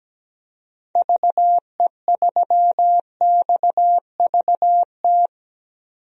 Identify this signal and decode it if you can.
Morse code, heard as a keyed tone, about 17 wpm, 700 Hz. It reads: VE3XVT
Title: DXCC-100 Call Signs - 6 Characters - 17wpm